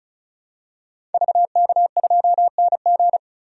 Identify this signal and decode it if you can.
Morse code, heard as a keyed tone, about 34 wpm, 700 Hz. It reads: VK2NG
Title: DXCC-100 Call Signs - 5 Characters - 35wpm